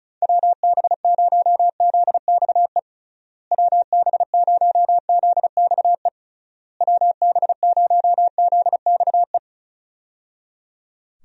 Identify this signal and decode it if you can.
Morse code, heard as a keyed tone, about 36 wpm, 700 Hz. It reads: WB0ZXE WB0ZXE WB0ZXE
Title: USA Call Signs - 6 Characters - Repeated 3x - 35wpm